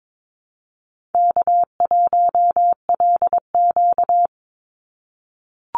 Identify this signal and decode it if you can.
Morse code, heard as a keyed tone, about 22 wpm, 700 Hz. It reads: K1LQ
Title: USA Call Signs 22wpm